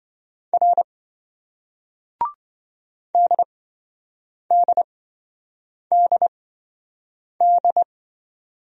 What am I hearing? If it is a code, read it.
Morse code, heard as a keyed tone, about 26 wpm, spaced as for 7 wpm, 700 Hz. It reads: R DDDD